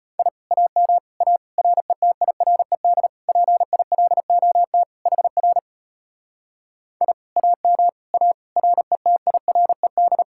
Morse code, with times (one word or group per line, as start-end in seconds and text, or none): 0.2-0.3 I
0.5-1.0 AM
1.2-1.4 A
1.6-3.1 RETIRED
3.3-4.8 PILOT
5.1-5.6 HR
7.0-7.1 I
7.4-7.9 AM
8.1-8.3 A
8.6-10.2 RETIRED